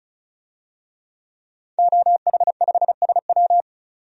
Morse code, heard as a keyed tone, about 36 wpm, 700 Hz. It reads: OH5SW
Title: DXCC-100 Call Signs - 5 Characters - 35wpm